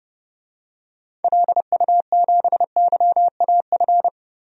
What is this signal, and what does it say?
Morse code, heard as a keyed tone, about 30 wpm, 700 Hz. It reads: LU7YAF
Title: DXCC-100 Call Signs - 6 Characters - Repeated 3x - 30wpm